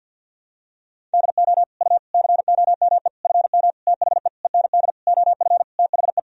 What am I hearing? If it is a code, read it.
Morse code, heard as a keyed tone, about 51 wpm, 700 Hz. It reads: DO U COME FM THE END OF THE
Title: Sentences from Top 200 Words and Abbreviations 50wpm